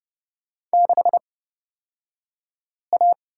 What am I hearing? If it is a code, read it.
Morse code, heard as a keyed tone, about 30 wpm, spaced as for 5 wpm, 700 Hz. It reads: 6A
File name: Lesson 37d - Letter x - Two Characters 30wpm